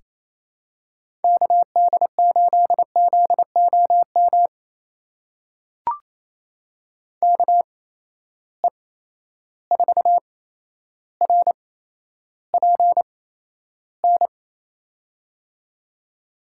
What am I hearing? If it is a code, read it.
Morse code, heard as a keyed tone, about 28 wpm, spaced as for 12 wpm, 700 Hz. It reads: KD8ZOM KE4RPN
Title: USA Call Signs - 6 Characters - 8x Character Spacing - 28wpm